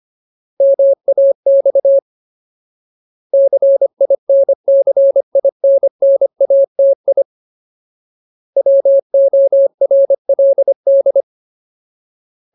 Morse code, heard as a keyed tone, about 25 wpm, 550 Hz. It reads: MAX CINCINNATI WORLD
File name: Ultimate Word List / 25WPM Ultimate